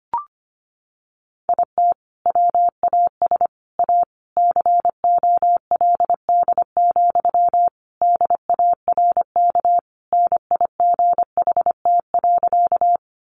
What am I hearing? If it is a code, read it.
Morse code, heard as a keyed tone, about 25 wpm, 700 Hz. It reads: IT WAS A COLD, DARK NIGHT.
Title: Sentences from Top 500 Words 25wpm